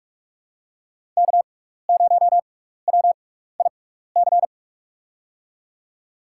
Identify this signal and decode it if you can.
Morse code, heard as a keyed tone, about 43 wpm, spaced as for 15 wpm, 700 Hz. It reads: K0WIC
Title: USA Call Signs - 5 Characters - 6x Character Spacing - 45wpm